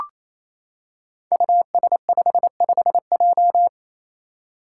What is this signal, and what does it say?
Morse code, heard as a keyed tone, about 28 wpm, 700 Hz. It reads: US55J
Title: Five Challenging Letter-Number Combinations 28wpm